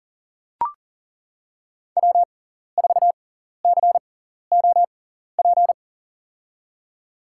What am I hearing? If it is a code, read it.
Morse code, heard as a keyed tone, about 40 wpm, spaced as for 14 wpm, 700 Hz. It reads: W4COP
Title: USA Call Signs - 5 Characters - 6x Character Spacing - 40wpm